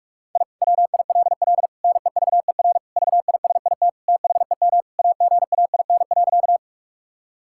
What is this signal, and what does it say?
Morse code, heard as a keyed tone, about 46 wpm, 700 Hz. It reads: I WILL NEVER VISIT THEM AGAIN.
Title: Sentences from Top 1000 Words - Review Part 6 - 45wpm